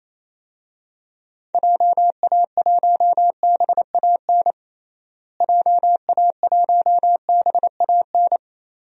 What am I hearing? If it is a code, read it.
Morse code, heard as a keyed tone, about 27 wpm, 700 Hz. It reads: JA1BAN JA1BAN
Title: DXCC-100 Call Signs - 6 Characters - Repeated 2x - 28wpm